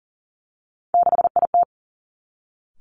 Morse code, heard as a keyed tone, about 41 wpm, 700 Hz. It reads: 6IT
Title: Three Challenging Letter-Number Combinations 40wpm